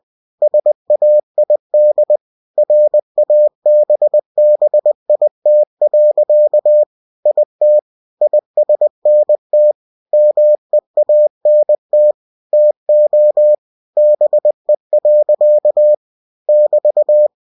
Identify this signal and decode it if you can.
Morse code, heard as a keyed tone, about 20 wpm, 600 Hz. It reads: SAID RABBIT. IT ISNT MEANT TO BE. <BT>
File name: cw-books / winnie-the-pooh / winnie-20wpm-page-0014